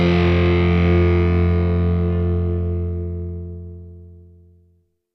Dial tone (North America), 350 + 440 Hz.